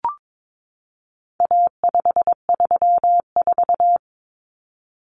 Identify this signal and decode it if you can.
Morse code, heard as a keyed tone, about 22 wpm, 700 Hz. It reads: A534
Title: Common Cut Numbers - 4-Digit 22wpm